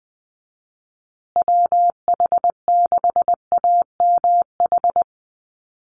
Morse code, heard as a keyed tone, about 20 wpm, 700 Hz. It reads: WH6AMH
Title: USA Call Signs Repeated 3x 20wpm